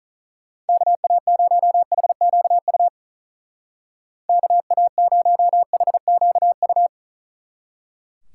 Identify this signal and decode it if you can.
Morse code, heard as a keyed tone, about 45 wpm, 700 Hz. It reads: KA0HQU KA0HQU